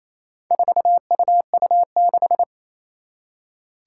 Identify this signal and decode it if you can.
Morse code, heard as a keyed tone, about 27 wpm, 700 Hz. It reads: 4UU6
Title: Four Challenging Letter-Number Combinations 28wpm